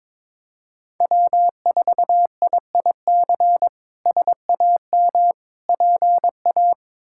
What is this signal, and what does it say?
Morse code, heard as a keyed tone, about 22 wpm, 700 Hz. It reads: W4IIC SAM PA
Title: K1USN SST QSO Practice - 22wpm